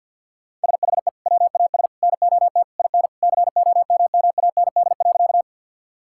Morse code, heard as a keyed tone, about 50 wpm, 700 Hz. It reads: SHE WAS NOT IN COMMAND.
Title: Sentences from Top 1600 Words 50wpm